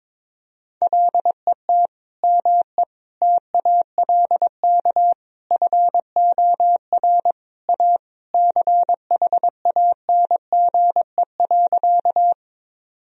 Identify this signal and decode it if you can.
Morse code, heard as a keyed tone, about 22 wpm, 700 Hz. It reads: LET ME TALK FOR A CHANGE.
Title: Sentences from Top 1000 Words - Review Part 1 - 22wpm